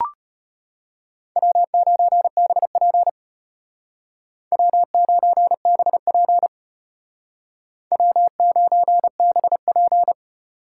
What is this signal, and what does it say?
Morse code, heard as a keyed tone, about 31 wpm, 700 Hz. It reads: W9BP W9BP W9BP
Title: Speed Racing - USA 1x2 Call Signs 25wpm